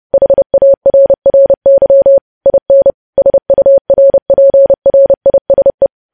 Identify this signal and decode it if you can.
Morse code, heard as a keyed tone, about 30 wpm, 550 Hz. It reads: HARRY IN SURPRISE
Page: news headlines in morse code at 30 wpm